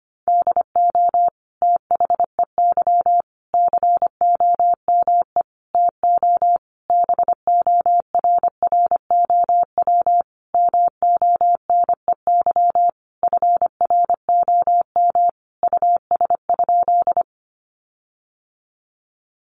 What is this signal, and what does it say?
Morse code, heard as a keyed tone, about 25 wpm, 700 Hz. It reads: DO THEY COME TO BORROW MONEY FROM US?
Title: Sentences from Top 1500 Words 25wpm